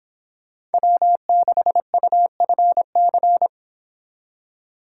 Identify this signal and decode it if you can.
Morse code, heard as a keyed tone, about 26 wpm, 700 Hz. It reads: W6UFC
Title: Speed Racing - USA 1x3 Call Signs 22wpm